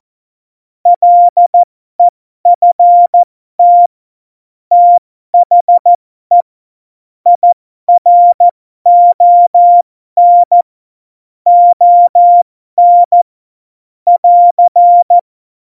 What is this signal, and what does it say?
Morse code, heard as a keyed tone, about 14 wpm, spaced as for 12 wpm, 700 Hz. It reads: LEFT THE IRON ON <AR>
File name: Morsekurs / Witze